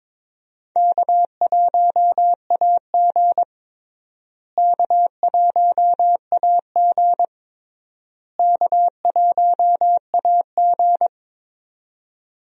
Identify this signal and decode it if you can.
Morse code, heard as a keyed tone, about 22 wpm, 700 Hz. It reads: K1AG K1AG K1AG
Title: USA Call Signs - 4 Characters - Repeated 3x - 22wpm